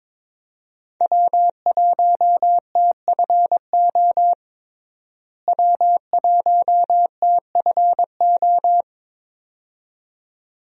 Morse code, heard as a keyed tone, about 22 wpm, 700 Hz. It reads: W1TFO W1TFO
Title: USA Call Signs - 5 Characters - Repeated 2x - 22wpm